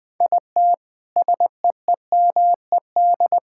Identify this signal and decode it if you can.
Morse code, heard as a keyed tone, about 20 wpm, 700 Hz. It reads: IT SEEMED
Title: Sentences with 8-Word Phrases - 20wpm